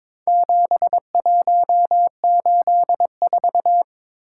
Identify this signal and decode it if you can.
Morse code, heard as a keyed tone, about 22 wpm, 700 Hz. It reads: 7184
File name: practice / 22wpm / Four Letter-Number Combinations - 22wpm